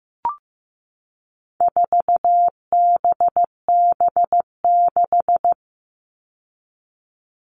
Morse code, heard as a keyed tone, about 15 wpm, 700 Hz. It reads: 4BB6
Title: All Cut Numbers - 4-Digit 15wpm